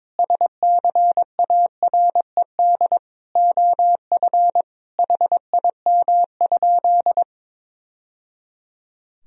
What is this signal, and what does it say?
Morse code, heard as a keyed tone, about 22 wpm, 700 Hz. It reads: SCARED OF HIM?
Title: Sentences from Top 1000 Words - Review Part 4 - 22wpm